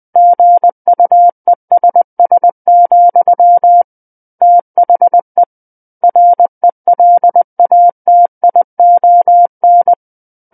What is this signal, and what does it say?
Morse code, heard as a keyed tone, about 20 wpm, 700 Hz. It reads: GUESS, THE RELATION
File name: 20 WPM